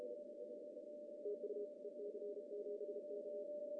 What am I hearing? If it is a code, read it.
Morse code, heard as a keyed tone, about 33 wpm, 410 Hz. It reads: TU EC7K
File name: CW / cqww-cw-k5tr